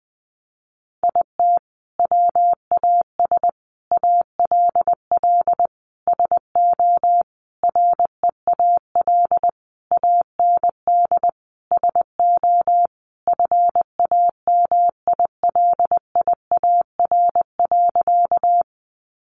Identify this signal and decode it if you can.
Morse code, heard as a keyed tone, about 20 wpm, 700 Hz. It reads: IT WAS ALL SO REAL AND SO FAMILIAR.
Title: Sentences from Top 1000 Words 20wpm